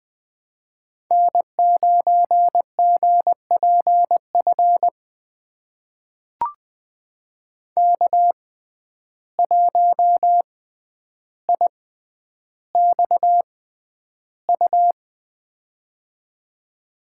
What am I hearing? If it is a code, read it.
Morse code, heard as a keyed tone, about 20 wpm, spaced as for 10 wpm, 700 Hz. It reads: N9GPF K1IXU
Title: USA Call Signs - 5 Characters - 6x Character Spacing - 20wpm